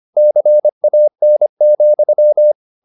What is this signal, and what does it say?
Morse code, heard as a keyed tone, about 25 wpm, 600 Hz. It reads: CAN,
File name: the old man and the sea part 1 25wpm